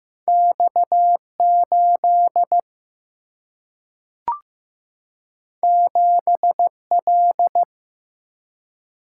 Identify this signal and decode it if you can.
Morse code, heard as a keyed tone, about 15 wpm, 700 Hz. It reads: X 8 7 L
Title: Single-Letter Single-Number 15wpm